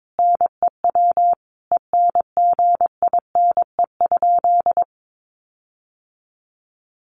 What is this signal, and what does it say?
Morse code, heard as a keyed tone, about 22 wpm, 700 Hz. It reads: NEW ENGINE?
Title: Sentences from Top 1500 Words 22wpm